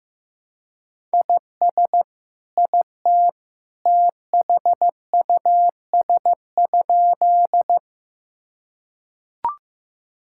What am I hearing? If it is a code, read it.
Morse code, heard as a keyed tone, about 16 wpm, 700 Hz. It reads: IS IT THUS?
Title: Sentences from Top 1000 Words - Review Part 5 - 15wpm